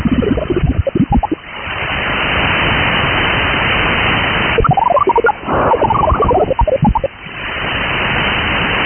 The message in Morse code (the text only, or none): none